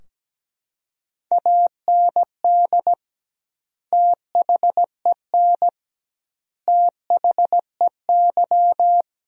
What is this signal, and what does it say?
Morse code, heard as a keyed tone, about 17 wpm, 700 Hz. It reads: AND THEN THEY
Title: 2x Word-Spacing - Sentences from Top 300 Words - 17wpm